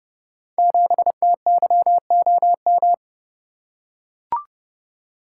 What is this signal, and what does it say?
Morse code, heard as a keyed tone, about 30 wpm, 700 Hz. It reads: 7TYOM
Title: Code Groups - 5 Letter:Numbers - 30wpm